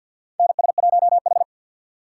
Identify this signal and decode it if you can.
Morse code, heard as a keyed tone, about 53 wpm, 700 Hz. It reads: NS1H
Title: USA Call Signs - 4 Characters - 3x Character Spacing - 50wpm